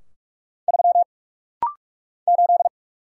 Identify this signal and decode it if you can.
Morse code, heard as a keyed tone, about 46 wpm, 700 Hz. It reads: 3 8
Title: Single Letter-Number - Rapid-Fire 45wpm